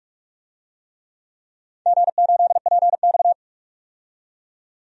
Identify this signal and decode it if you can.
Morse code, heard as a keyed tone, about 44 wpm, 690 Hz. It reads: G8PX
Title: DXCC-100 Call Signs - 4 Characters - 45wpm